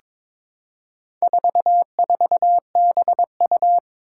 Morse code, heard as a keyed tone, about 22 wpm, 700 Hz. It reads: 44BU
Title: Four Challenging Letter-Number Combinations 22wpm